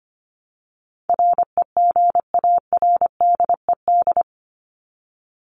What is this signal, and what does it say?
Morse code, heard as a keyed tone, about 25 wpm, 700 Hz. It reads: REGARDED
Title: ED Suffix - Top 100 Words with Base Words 25wpm